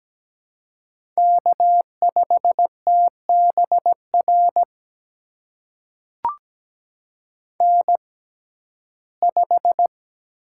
Morse code, heard as a keyed tone, about 17 wpm, spaced as for 12 wpm, 700 Hz. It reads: K5TBR N5
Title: USA Call Signs - 5 Characters - 6x Character Spacing - 17wpm